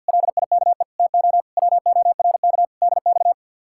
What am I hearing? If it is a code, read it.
Morse code, heard as a keyed tone, about 63 wpm, 700 Hz. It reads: LIKE TO WORK DX